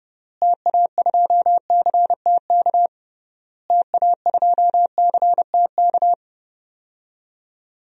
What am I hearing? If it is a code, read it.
Morse code, heard as a keyed tone, about 30 wpm, 700 Hz. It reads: TA2CTK TA2CTK
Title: DXCC-100 Call Signs - 6 Characters - Repeated 3x - 30wpm